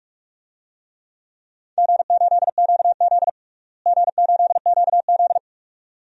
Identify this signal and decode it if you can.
Morse code, heard as a keyed tone, about 42 wpm, 700 Hz. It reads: G8QZ G8QZ
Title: DXCC-100 Call Signs - 4 Characters - Repeated 2x - 45wpm